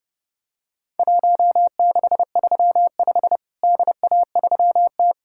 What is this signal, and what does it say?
Morse code, heard as a keyed tone, about 32 wpm, 700 Hz. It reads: 1635 DA3T
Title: Worked All Europe DX Contest - QTC Exchange - 30wpm